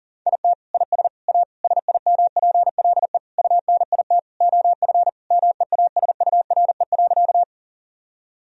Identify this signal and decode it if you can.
Morse code, heard as a keyed tone, about 40 wpm, 700 Hz. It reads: IT IS A SIMPLE UNIT OF MEASURE.
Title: Sentences from Top 1000 Words - Review Part 2 - 40wpm